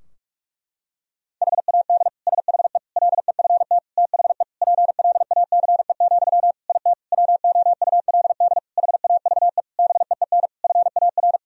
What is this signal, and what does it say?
Morse code, heard as a keyed tone, about 42 wpm, 700 Hz. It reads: HAD SHE LEFT THE PLACE, IT WOULD HAVE BEEN FAR
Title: Sentences from Top 1000 Words - Review Part 3 - 45wpm